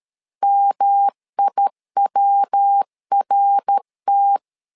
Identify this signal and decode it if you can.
Morse code, heard as a keyed tone, about 13 wpm, 800 Hz. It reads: MIWRT
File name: LEZIONI / 4 Q 12wpm 1833